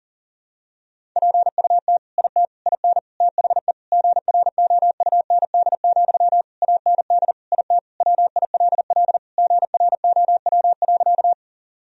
Morse code, heard as a keyed tone, about 40 wpm, 700 Hz. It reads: PUT IT IN THE GROUND, AND IT WILL GROW.
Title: Sentences from Top 1000 Words - Review Part 2 - 40wpm